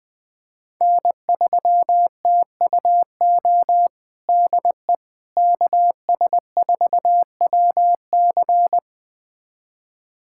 Morse code, heard as a keyed tone, about 20 wpm, 700 Hz. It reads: N3TUO DE KS4WC
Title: USA Callsign DE Callsign - 5 Characters - 20wpm